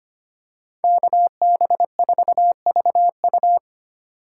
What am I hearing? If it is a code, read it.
Morse code, heard as a keyed tone, about 25 wpm, 700 Hz. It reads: KB4VU
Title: K1USN SST QSO Practice - 25wpm